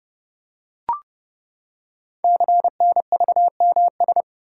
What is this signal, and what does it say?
Morse code, heard as a keyed tone, about 30 wpm, 700 Hz. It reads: CNVMS
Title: Code Groups - 5 Letters - 30wpm